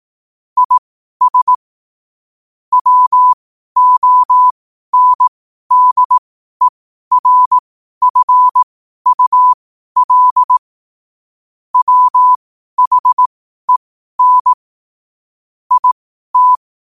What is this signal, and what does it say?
Morse code, heard as a keyed tone, about 18 wpm, spaced as for 12 wpm, 1000 Hz. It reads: IS WONDERFUL WHEN IT
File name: qotdfiles / 12wpm